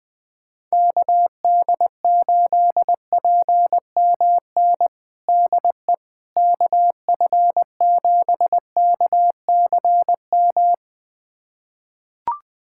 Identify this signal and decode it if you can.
Morse code, heard as a keyed tone, about 20 wpm, 700 Hz. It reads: KD8PMN DE KF7KCM
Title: USA Callsign DE Callsign - 6 Characters - 20wpm